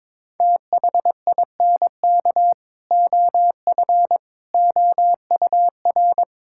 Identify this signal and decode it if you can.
Morse code, heard as a keyed tone, about 22 wpm, 700 Hz. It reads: THINK OF OUR
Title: Sentences from Top 1600 Words 22wpm